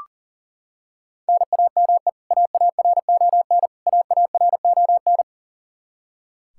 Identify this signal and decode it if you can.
Morse code, heard as a keyed tone, about 37 wpm, 700 Hz. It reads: NAME AARON AARON
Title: Practice QSO - Exchange 1 - Beginning - 40wpm